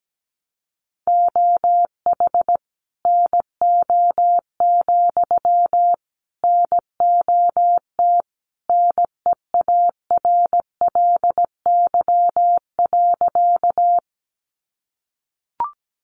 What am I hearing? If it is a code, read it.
Morse code, heard as a keyed tone, about 17 wpm, 700 Hz. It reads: OH NO, NOT NEARLY.
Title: Sentences from Top 1300 Words 17wpm